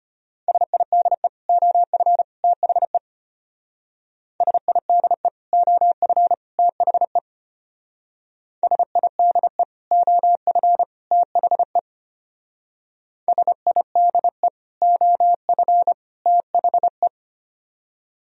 Morse code, heard as a keyed tone, about 31 wpm, 700 Hz. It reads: SIDE OF THE SIDE OF THE SIDE OF THE SIDE OF THE
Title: Speed-Racing - Sets of 3 Words 25wpm